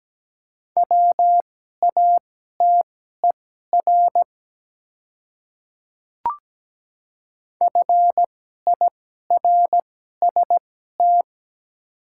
Morse code, heard as a keyed tone, about 17 wpm, spaced as for 12 wpm, 700 Hz. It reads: WATER FIRST
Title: 2x Character Spacing - Top 300 Words - 17wpm